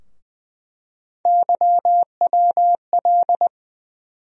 Morse code, heard as a keyed tone, about 20 wpm, 700 Hz. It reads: YWL
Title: USA Call Sign Postfixes - 3 Letters 20wpm